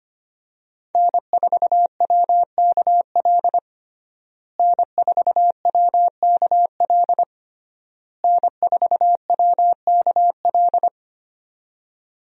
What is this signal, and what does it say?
Morse code, heard as a keyed tone, about 25 wpm, 700 Hz. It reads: N4WKL N4WKL N4WKL
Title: USA Call Signs - 5 Characters - Repeated 3x - 25wpm